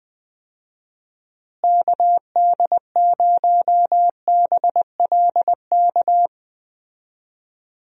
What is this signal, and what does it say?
Morse code, heard as a keyed tone, about 20 wpm, 700 Hz. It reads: KD0BLK
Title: USA Call Signs - 6 Characters - Repeated 3x - 20wpm